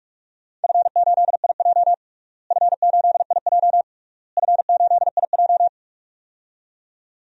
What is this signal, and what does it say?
Morse code, heard as a keyed tone, about 42 wpm, 700 Hz. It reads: F8IJ F8IJ F8IJ
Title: DXCC-100 Call Signs - 4 Characters - Repeated 3x - 45wpm